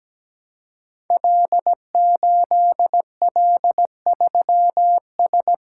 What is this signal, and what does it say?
Morse code, heard as a keyed tone, about 17 wpm, 700 Hz. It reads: L8L3S
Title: Code Groups - 5 Letter:Numbers - 17wpm